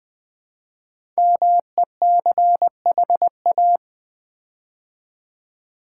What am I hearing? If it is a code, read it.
Morse code, heard as a keyed tone, about 20 wpm, 700 Hz. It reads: MECHA
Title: Build A Word - 5 Letters At A Time - 10-Letter Words - 20wpm